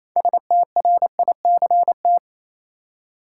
Morse code, heard as a keyed tone, about 27 wpm, 700 Hz. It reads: STRICT
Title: Six Letter Words - 28wpm